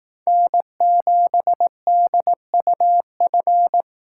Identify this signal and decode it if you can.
Morse code, heard as a keyed tone, about 18 wpm, 700 Hz. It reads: N7DUF
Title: Speed Racing - USA 1x3 Call Signs 15wpm